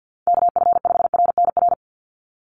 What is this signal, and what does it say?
Morse code, heard as a keyed tone, about 50 wpm, 700 Hz. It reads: KF5RNR
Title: USA Call Signs 50wpm